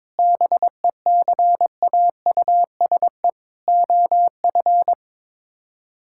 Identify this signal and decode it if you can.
Morse code, heard as a keyed tone, about 22 wpm, 700 Hz. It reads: BECAUSE OF
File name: Sets of 9 Words - 22wpm